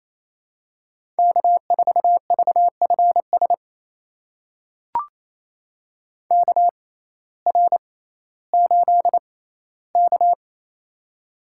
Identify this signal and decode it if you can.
Morse code, heard as a keyed tone, about 28 wpm, spaced as for 15 wpm, 700 Hz. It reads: K4VFS KR8K
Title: USA Call Signs - 6x Character Spacing - 28wpm